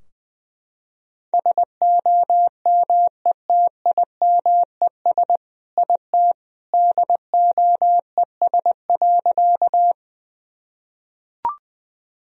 Morse code, heard as a keyed tone, about 20 wpm, 700 Hz. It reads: SOMETIMES IT DOES.
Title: Sentences from Top 1000 Words - Review Part 3 - 20wpm